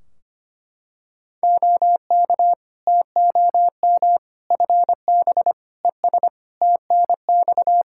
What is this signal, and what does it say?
Morse code, heard as a keyed tone, about 25 wpm, 700 Hz. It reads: OK TOM FB ES TNX
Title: Practice QSO - Exchange 2 - Info - 25wpm